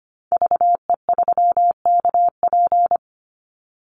Morse code, heard as a keyed tone, about 25 wpm, 700 Hz. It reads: VE3KP
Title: Call Signs Repeated 3x - 25wpm